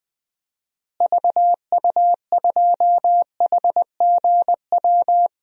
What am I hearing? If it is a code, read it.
Morse code, heard as a keyed tone, about 20 wpm, 700 Hz. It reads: VU2HGW
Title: DXCC-100 Call Signs - 6 Characters - Repeated 2x - 20wpm